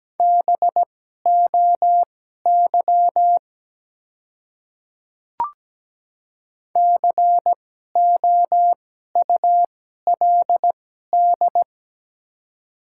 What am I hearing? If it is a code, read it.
Morse code, heard as a keyed tone, about 17 wpm, spaced as for 12 wpm, 700 Hz. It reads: BOY COULD
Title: 2x Character Spacing - Top 200 Words - 17wpm